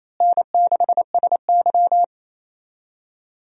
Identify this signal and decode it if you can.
Morse code, heard as a keyed tone, about 28 wpm, 700 Hz. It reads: N6SY
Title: USA Call Signs - 4 Characters - Repeated 3x - 28wpm